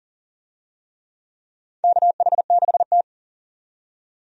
Morse code, heard as a keyed tone, about 40 wpm, 700 Hz. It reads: KH6T
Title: USA Call Signs - 4 Characters - 40wpm